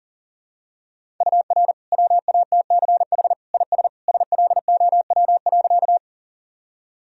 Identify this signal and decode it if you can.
Morse code, heard as a keyed tone, about 40 wpm, 700 Hz. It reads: UR WATCH IS SLOW.
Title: Sentences from Top 400 Words and Abbreviations 40wpm